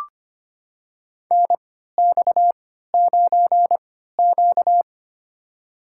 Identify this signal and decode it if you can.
Morse code, heard as a keyed tone, about 25 wpm, spaced as for 14 wpm, 700 Hz. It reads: NX9Q
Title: USA Call Signs - 4 Characters - 3x Character Spacing - 25wpm